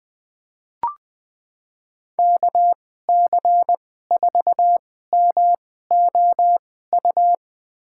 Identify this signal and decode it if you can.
Morse code, heard as a keyed tone, about 20 wpm, spaced as for 14 wpm, 700 Hz. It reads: KC4MOU
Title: USA Call Signs - 2x Character Spacing - 20wpm